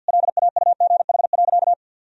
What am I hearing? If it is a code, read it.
Morse code, heard as a keyed tone, about 48 wpm, 700 Hz. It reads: LAUGH.
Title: Sentences from Top 1000 Words - Review Part 2 - 50wpm